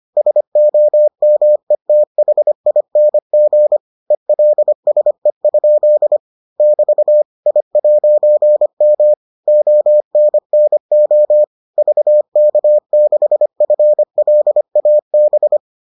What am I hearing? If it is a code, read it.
Morse code, heard as a keyed tone, about 25 wpm, 600 Hz. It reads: SOMETHING ELSE? <BT> I'M ONNO VK6FLAB